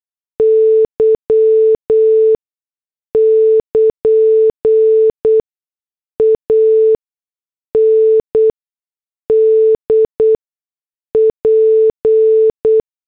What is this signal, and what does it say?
Morse code, heard as a keyed tone, about 8 wpm, spaced as for 6 wpm, 440 Hz. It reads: Y <KN> AND P